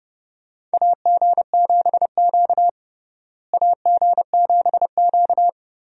Morse code, heard as a keyed tone, about 30 wpm, 700 Hz. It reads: AG7Q AG7Q
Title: USA Call Signs - 4 Characters - Repeated 2x - 30wpm